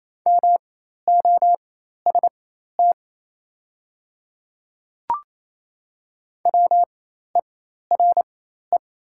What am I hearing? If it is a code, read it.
Morse code, heard as a keyed tone, about 28 wpm, spaced as for 13 wpm, 700 Hz. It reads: MOST WERE
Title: 4x Character Spacing - Top 100 Words - 28wpm